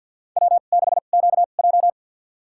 Morse code, heard as a keyed tone, about 53 wpm, spaced as for 37 wpm, 700 Hz. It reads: W6QJ
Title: USA Call Signs - 4 Characters - 2x Character Spacing - 50wpm